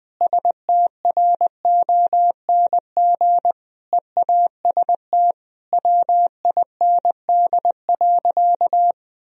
Morse code, heard as a keyed tone, about 20 wpm, 700 Hz. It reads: STRONG EAST WIND.
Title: Sentences from Top 1000 Words - Review Part 3 - 20wpm